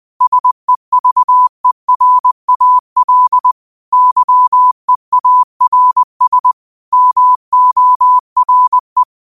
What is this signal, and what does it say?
Morse code, heard as a keyed tone, about 20 wpm, 1000 Hz. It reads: SEVERAL YEARS MORE
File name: qotdfiles / 20wpm